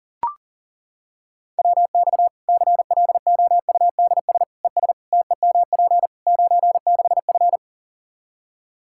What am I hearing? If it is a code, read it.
Morse code, heard as a keyed tone, about 40 wpm, 700 Hz. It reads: WX CLOUDS ES TEMP 96F
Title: Practice QSO - Exchange 2 - Info - 40wpm